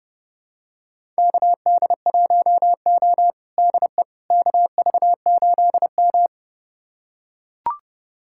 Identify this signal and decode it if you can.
Morse code, heard as a keyed tone, about 31 wpm, 700 Hz. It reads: KD1O DE KV8M
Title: USA Callsign DE Callsign - 4 Characters - 30wpm